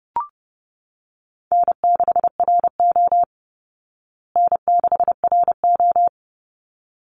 Morse code, heard as a keyed tone, about 30 wpm, 700 Hz. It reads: N6RO N6RO
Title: Call Signs Repeated 3x - 30wpm